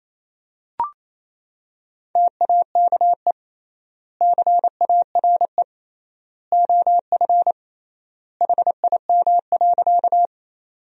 Morse code, heard as a keyed tone, about 28 wpm, 700 Hz. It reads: TAKE CARE OF HIM.
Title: 3x Word-Spacing - Sentences from Top 300 Words - 28wpm